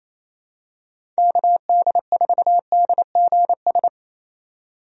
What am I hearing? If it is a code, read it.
Morse code, heard as a keyed tone, about 28 wpm, 700 Hz. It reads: KD4DGS
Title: USA Call Signs - 6 Characters - 28wpm